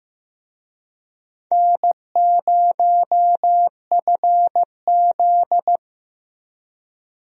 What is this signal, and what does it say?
Morse code, heard as a keyed tone, about 15 wpm, 700 Hz. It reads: N0FZ
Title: USA Call Signs - 4 Characters - Repeated 2x - 15wpm